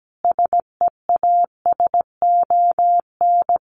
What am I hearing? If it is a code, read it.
Morse code, heard as a keyed tone, about 17 wpm, 700 Hz. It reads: SEASON
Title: Sentences from Top 1400 Words 17wpm